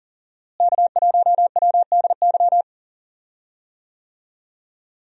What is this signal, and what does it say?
Morse code, heard as a keyed tone, about 40 wpm, 700 Hz. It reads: K1WDY
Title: USA Call Signs - 5 Characters - Repeated 2x - 40wpm